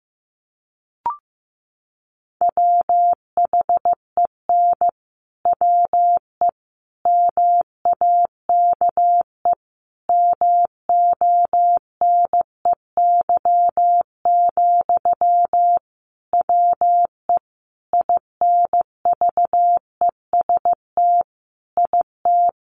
Morse code, heard as a keyed tone, about 15 wpm, 700 Hz. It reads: WHEN WE MAKE MONEY, WE INVEST IT
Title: Sentences from Top 1100 Words 15wpm